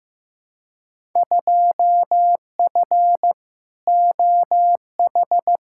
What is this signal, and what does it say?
Morse code, heard as a keyed tone, about 15 wpm, 700 Hz. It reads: 2F OH
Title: Field Day Exchanges - 15wpm